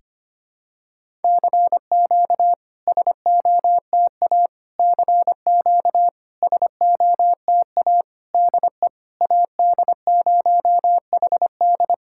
Morse code, heard as a keyed tone, about 25 wpm, 700 Hz. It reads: CQ SOTA CQ SOTA DE AD0HD
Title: SOTA QSO Practice 25wpm